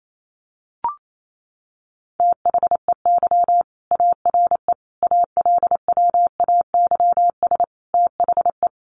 Morse code, heard as a keyed tone, about 28 wpm, 700 Hz. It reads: THEY ARE ALWAYS THE
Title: Sentences from Top 300 Words 28wpm